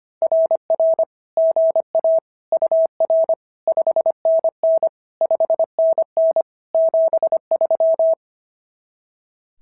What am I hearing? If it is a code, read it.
Morse code, heard as a keyed tone, about 24 wpm, 650 Hz. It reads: RR GA UR 5NN 5NN 73
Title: SOTA QSO Practice 25wpm